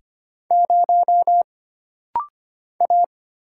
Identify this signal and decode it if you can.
Morse code, heard as a keyed tone, about 25 wpm, 700 Hz. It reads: 0 A
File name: Single Character (Letter:Number:Punctuation:Prosign) - Rapid-Fire - 25wpm